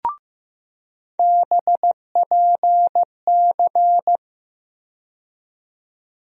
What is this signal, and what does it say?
Morse code, heard as a keyed tone, about 15 wpm, 700 Hz. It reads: BPC
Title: USA Call Sign Postfixes - 3 Letters 15wpm